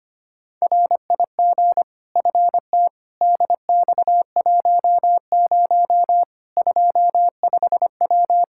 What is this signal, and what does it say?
Morse code, heard as a keyed tone, about 25 wpm, 700 Hz. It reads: RIG FT DX10 25W
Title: Practice QSO - Exchange 2 - Info - 25wpm